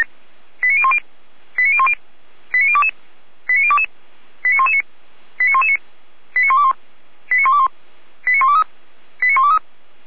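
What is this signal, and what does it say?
Five-tone selective call (ZVEI), heard as quick runs of five tones, tones of about 70 ms. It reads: 89010 89010 89022 89022 89119 89119 89121 89121 89123 89123
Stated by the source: Received on an SDR; ZVEI five-tone call.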